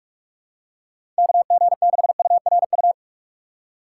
Morse code, heard as a keyed tone, about 46 wpm, 700 Hz. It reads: KG6URU
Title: USA Call Signs - 6x Character Spacing - 45wpm